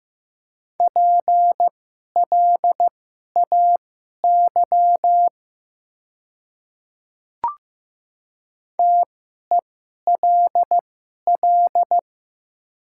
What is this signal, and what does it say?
Morse code, heard as a keyed tone, about 15 wpm, spaced as for 11 wpm, 700 Hz. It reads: PLAY TELL
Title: 2x Character Spacing - Top 500 Words - 15wpm